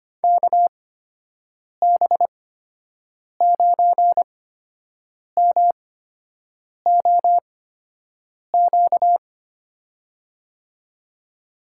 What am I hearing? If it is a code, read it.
Morse code, heard as a keyed tone, about 25 wpm, spaced as for 7 wpm, 700 Hz. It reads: KB9MOQ
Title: USA Call Signs - 8x Character Spacing - 25wpm